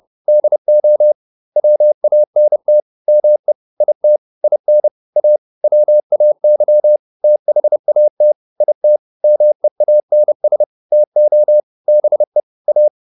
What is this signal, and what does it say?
Morse code, heard as a keyed tone, about 30 wpm, 600 Hz. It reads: DO WANT ME IT IN A WAY THAT IT MEANS TO BE A